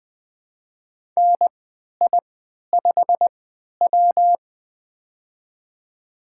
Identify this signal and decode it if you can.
Morse code, heard as a keyed tone, about 20 wpm, spaced as for 11 wpm, 700 Hz. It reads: NI5W
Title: USA Call Signs - 4 Characters - 3x Character Spacing - 20wpm